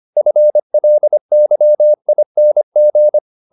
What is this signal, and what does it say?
Morse code, heard as a keyed tone, about 25 wpm, 600 Hz. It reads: FLYING
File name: the old man and the sea part 1 25wpm